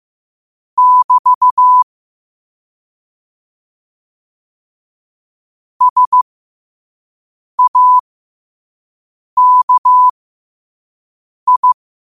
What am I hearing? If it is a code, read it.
Morse code, heard as a keyed tone, about 15 wpm, spaced as for 5 wpm, 1000 Hz. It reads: <BT> SAKI
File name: qotdfiles / 5wpm